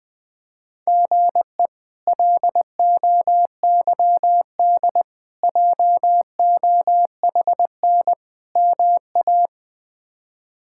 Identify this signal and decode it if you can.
Morse code, heard as a keyed tone, about 20 wpm, 700 Hz. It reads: GE LOYD JOHN MA
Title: K1USN SST QSO Practice - 20wpm